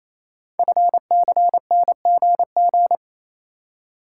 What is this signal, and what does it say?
Morse code, heard as a keyed tone, about 28 wpm, 700 Hz. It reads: FCNGG
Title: Code Groups - 5 Letters - 28wpm